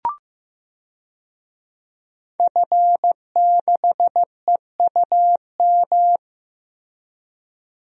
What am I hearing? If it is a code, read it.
Morse code, heard as a keyed tone, about 15 wpm, 700 Hz. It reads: F6EUM
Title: DXCC-100 Call Signs - 5 Characters - Repeated 2x - 15wpm